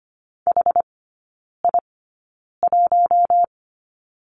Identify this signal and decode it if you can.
Morse code, heard as a keyed tone, about 25 wpm, spaced as for 9 wpm, 690 Hz. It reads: HI1
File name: Lesson 14e - Number 2 - Three Characters 25wpm